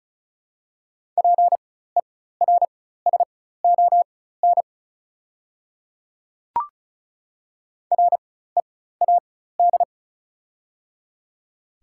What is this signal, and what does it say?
Morse code, heard as a keyed tone, about 36 wpm, spaced as for 16 wpm, 700 Hz. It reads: PERSON READ
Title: 4x Character Spacing - Top 500 Words - 35wpm